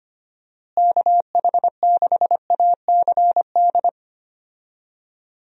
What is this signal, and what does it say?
Morse code, heard as a keyed tone, about 25 wpm, 700 Hz. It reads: KH6ACD
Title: USA Call Signs - 6 Characters - Repeated 3x - 25wpm